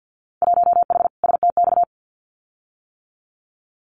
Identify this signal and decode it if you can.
Morse code, heard as a keyed tone, about 50 wpm, 700 Hz. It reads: 1H STX
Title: Winter Field Day Exchanges 50wpm